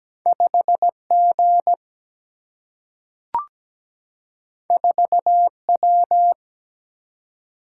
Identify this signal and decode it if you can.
Morse code, heard as a keyed tone, about 17 wpm, 700 Hz. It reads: 5 G 4 W
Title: Single-Letter Single-Number 17wpm